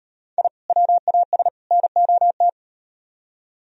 Morse code, heard as a keyed tone, about 37 wpm, 700 Hz. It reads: I WAS NOT